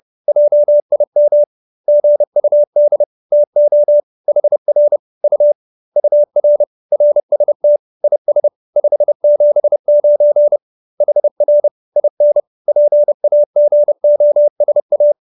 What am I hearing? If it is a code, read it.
Morse code, heard as a keyed tone, about 30 wpm, 590 Hz. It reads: JIM GUD TO HR U UR RST IS 579 HR IN PAGOSA